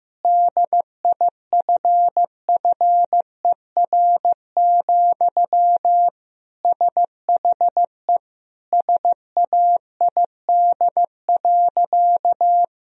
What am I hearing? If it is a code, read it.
Morse code, heard as a keyed tone, about 15 wpm, 700 Hz. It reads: DIFFER, SHE SAID.
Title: Sentences from Top 1000 Words - Review Part 1 - 15wpm